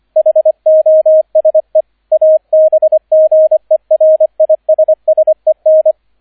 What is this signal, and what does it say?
Morse code, heard as a keyed tone, about 25 wpm, 630 Hz, with radio noise. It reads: HOSE ABGERISSEN